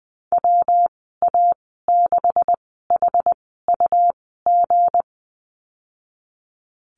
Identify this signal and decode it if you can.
Morse code, heard as a keyed tone, about 19 wpm, spaced as for 14 wpm, 700 Hz. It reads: WA6HUG
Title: 193 Lesson 31g - Letter v - Call Signs 20wpm